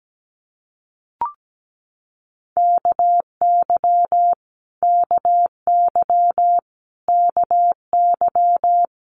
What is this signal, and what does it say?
Morse code, heard as a keyed tone, about 17 wpm, 700 Hz. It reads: KY KY KY
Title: Multi-letter Phonemes Repeated 3x 17wpm